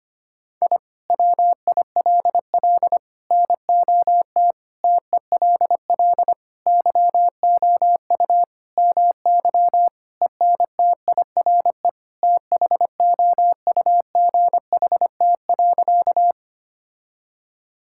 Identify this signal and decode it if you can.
Morse code, heard as a keyed tone, about 25 wpm, 700 Hz. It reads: I WILL NOT TELL YOU MY ENTIRE THOUGHT.
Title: Sentences from Top 1000 Words - Review Part 5 - 25wpm